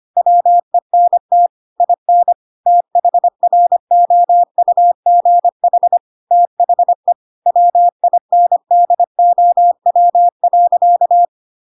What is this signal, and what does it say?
Morse code, heard as a keyed tone, about 25 wpm, 700 Hz. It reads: WENT IN THROUGH THE WINDOW.
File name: Pinocchio / Pinocchio ch7 25 25 ik1hge